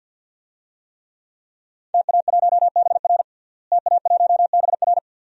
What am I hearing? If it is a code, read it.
Morse code, heard as a keyed tone, about 51 wpm, 700 Hz. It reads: TA1BR TA1BR
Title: DXCC-100 Call Signs - 5 Characters - Repeated 2x - 50wpm